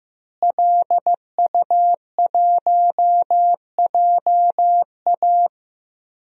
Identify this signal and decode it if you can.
Morse code, heard as a keyed tone, about 15 wpm, 700 Hz. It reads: LU1JA
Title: DXCC-100 Call Signs - 5 Characters - Repeated 3x - 15wpm